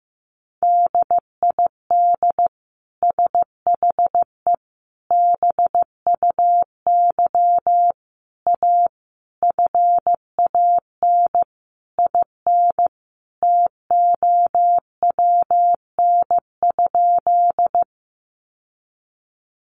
Morse code, heard as a keyed tone, about 15 wpm, 700 Hz. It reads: DID SHE BUY A FAN IN TOWN?
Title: Sentences from Top 1300 Words 15wpm